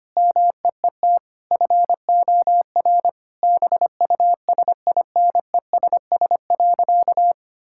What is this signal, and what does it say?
Morse code, heard as a keyed tone, about 25 wpm, 700 Hz. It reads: MEET FOR BUSINESS.
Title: Sentences from Top 1000 Words - Review Part 6 - 25wpm